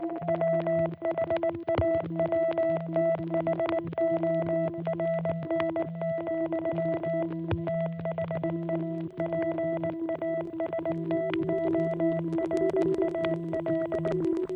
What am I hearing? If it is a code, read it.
Morse code, heard as a keyed tone, about 19 wpm, 660 Hz, with noise. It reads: WHAWTHORN, THE PASTORAL